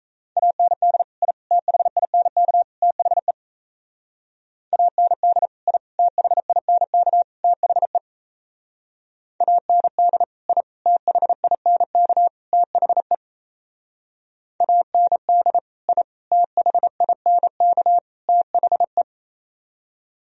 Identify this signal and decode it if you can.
Morse code, heard as a keyed tone, about 37 wpm, 700 Hz. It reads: AND I THINK THE AND I THINK THE AND I THINK THE AND I THINK THE